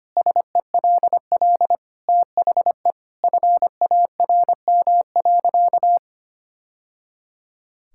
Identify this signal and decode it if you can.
Morse code, heard as a keyed tone, about 25 wpm, 700 Hz. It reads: SELL THE FARM.
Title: Sentences from Top 1000 Words - Review Part 6 - 25wpm